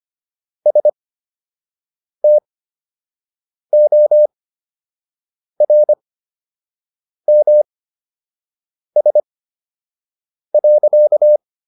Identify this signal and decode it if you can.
Morse code, heard as a keyed tone, about 25 wpm, spaced as for 6 wpm, 600 Hz. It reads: STORMS.